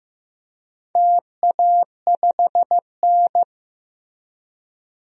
Morse code, heard as a keyed tone, about 15 wpm, 700 Hz. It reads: TA5N
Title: DXCC-100 Call Signs - 4 Characters - Repeated 3x - 15wpm